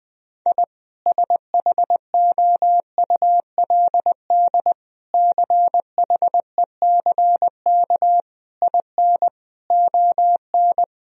Morse code, heard as a keyed tone, about 21 wpm, 700 Hz. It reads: I SHOULD CHECK IN ON
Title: Sentences from Top 1000 Words - Review Part 3 - 20wpm